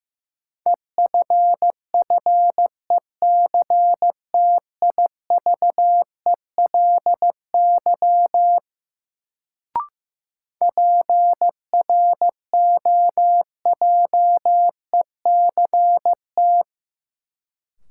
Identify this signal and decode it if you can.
Morse code, heard as a keyed tone, about 15 wpm, 700 Hz. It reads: EFFECTIVELY PROJECT
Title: Sentences from Top 1400 Words - New Word Drill - 15wpm